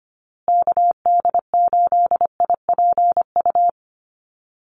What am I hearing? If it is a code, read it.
Morse code, heard as a keyed tone, about 25 wpm, 700 Hz. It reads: KD8IPU
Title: USA Call Signs Repeated 2x 25wpm